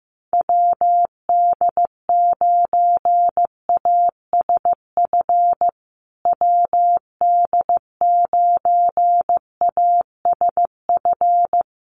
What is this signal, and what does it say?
Morse code, heard as a keyed tone, about 15 wpm, 700 Hz. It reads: WD9ASF WD9ASF
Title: USA Call Signs Repeated 2x 15wpm